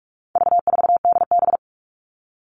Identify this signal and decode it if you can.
Morse code, heard as a keyed tone, about 44 wpm, 700 Hz. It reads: V4DB
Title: All Cut Numbers - 4-Digit 45wpm